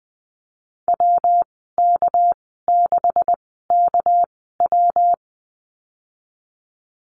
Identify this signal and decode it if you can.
Morse code, heard as a keyed tone, about 20 wpm, spaced as for 15 wpm, 700 Hz. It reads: WK6KW
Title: 231 Lesson 37g - Letter x - Call Signs 20wpm